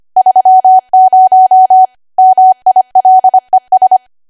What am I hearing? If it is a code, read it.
Morse code, heard as a keyed tone, about 25 wpm, 750 Hz. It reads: 30 MILES
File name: Archive / 25 WPM / 25WPM